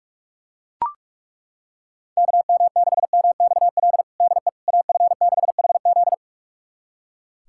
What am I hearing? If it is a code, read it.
Morse code, heard as a keyed tone, about 42 wpm, 690 Hz. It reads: KM6MXL DE AF6HZ
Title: USA Callsign DE Callsign 45wpm